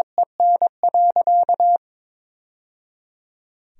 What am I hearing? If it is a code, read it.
Morse code, heard as a keyed tone, about 21 wpm, 700 Hz. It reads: EN.
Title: Sentences from Top 1000 Words - Review Part 2 - 22wpm